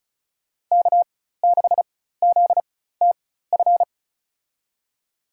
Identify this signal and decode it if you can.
Morse code, heard as a keyed tone, about 34 wpm, spaced as for 16 wpm, 700 Hz. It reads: K6ZTF
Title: USA Call Signs - 5 Characters - 4x Character Spacing - 35wpm